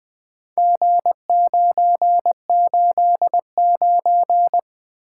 Morse code, heard as a keyed tone, about 20 wpm, 700 Hz. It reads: G989
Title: Four Letter-Number Combinations - 20wpm